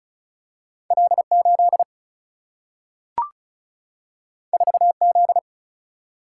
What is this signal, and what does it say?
Morse code, heard as a keyed tone, about 36 wpm, 700 Hz. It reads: L 8 4 Z
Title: Single-Letter Single-Number 35wpm